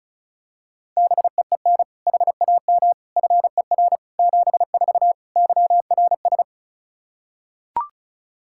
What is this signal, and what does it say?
Morse code, heard as a keyed tone, about 36 wpm, 700 Hz. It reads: BEEN HAM FER 74 YRS
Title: Practice QSO - Exchange 3 - More Info - 35wpm